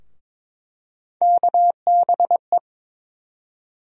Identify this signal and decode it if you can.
Morse code, heard as a keyed tone, about 22 wpm, 700 Hz. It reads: KBE
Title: USA Call Sign Postfixes - 3 Letters 22wpm